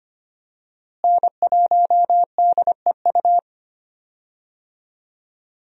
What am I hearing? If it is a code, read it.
Morse code, heard as a keyed tone, about 25 wpm, 700 Hz. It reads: N1DEU
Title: USA Call Signs - 5 Characters - Repeated 2x - 25wpm